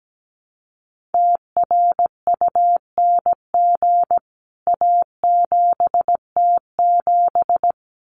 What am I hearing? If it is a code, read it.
Morse code, heard as a keyed tone, about 17 wpm, 700 Hz. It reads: TRUNG A7T7
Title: CWT Exchange 17wpm